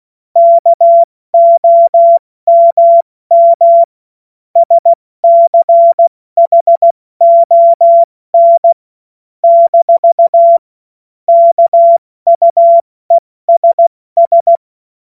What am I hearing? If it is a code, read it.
Morse code, heard as a keyed tone, about 16 wpm, spaced as for 14 wpm, 680 Hz. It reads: KOMM SCHON - KUESS
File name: Morsekurs / Witze / Witz 08. November 2022 70 F80